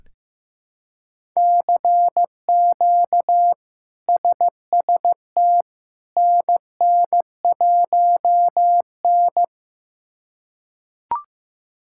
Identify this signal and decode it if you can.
Morse code, heard as a keyed tone, about 15 wpm, 700 Hz. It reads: CQ SST NN1N
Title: K1USN SST QSO Practice - 15wpm